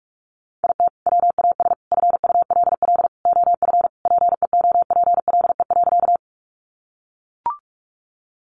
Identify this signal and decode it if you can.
Morse code, heard as a keyed tone, about 44 wpm, 700 Hz. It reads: IT WAS FULL OF PEOPLE.
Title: Sentences from Top 500 Words 45wpm